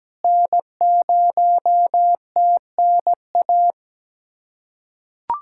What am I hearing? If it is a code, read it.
Morse code, heard as a keyed tone, about 17 wpm, 700 Hz. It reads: N0TNA
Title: USA Callsign DE Callsign - 5 Characters - 17wpm